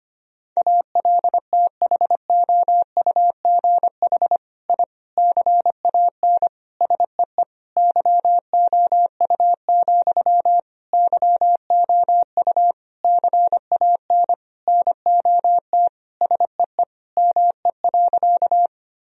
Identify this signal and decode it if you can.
Morse code, heard as a keyed tone, about 25 wpm, 700 Hz. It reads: ALTHOUGH I CAN SEE YOU, YOU CAN NOT SEE ME.
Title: Sentences from Top 1000 Words - Review Part 3 - 25wpm